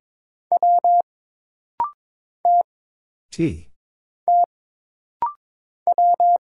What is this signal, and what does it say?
Morse code, heard as a keyed tone, about 22 wpm, 700 Hz. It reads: W T T W
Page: This rapid-fire timing is geared to push your ICR (Instant Character Recognition) skills.